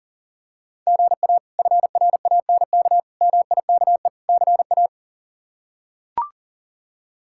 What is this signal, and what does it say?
Morse code, heard as a keyed tone, about 40 wpm, 700 Hz. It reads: GA FRANK MIKE CA
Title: K1USN SST QSO Practice - 40wpm